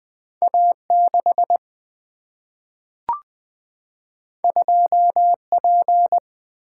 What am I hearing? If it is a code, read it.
Morse code, heard as a keyed tone, about 20 wpm, 700 Hz. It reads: A 6 2 P
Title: Single-Letter Single-Number 20wpm